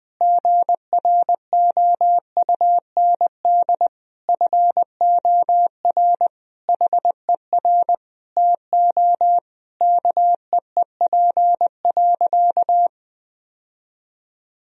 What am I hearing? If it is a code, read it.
Morse code, heard as a keyed tone, about 20 wpm, 700 Hz. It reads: GROUND FOR HER TO KEEP.
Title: Sentences from Top 1000 Words - Review Part 2 - 20wpm